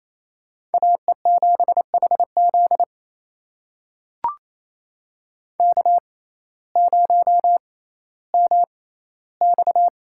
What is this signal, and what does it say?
Morse code, heard as a keyed tone, about 28 wpm, spaced as for 15 wpm, 700 Hz. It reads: AE7HZ K0MX
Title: USA Call Signs - 6x Character Spacing - 28wpm